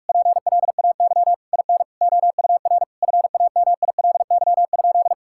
Morse code, heard as a keyed tone, about 46 wpm, 700 Hz. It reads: PLAY IN OUR FAMILY?
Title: Sentences from Top 1000 Words - Review Part 6 - 45wpm